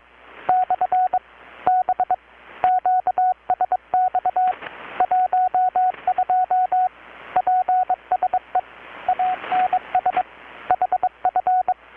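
Morse code, heard as a keyed tone, about 22 wpm, 720 Hz, with radio noise. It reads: / B QSX 12 PSE ANS HF
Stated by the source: Morse from KPH's Night of Nights XX maritime radiotelegraphy event, received in Chiapas, Mexico about 3,800 km away on a Kenwood R-600 with a grounded 10 m long-wire antenna.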